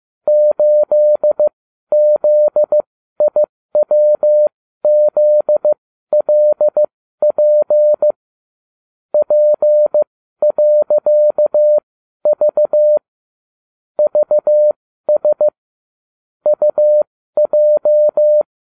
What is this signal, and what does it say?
Morse code, heard as a keyed tone, about 15 wpm, spaced as for 12 wpm, 600 Hz. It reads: 8ZIWZLP P.V VS UJ
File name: cw / ubacw29 2